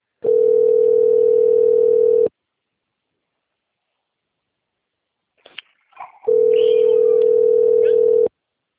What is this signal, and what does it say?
Ringback tone (North America), 440 + 480 Hz.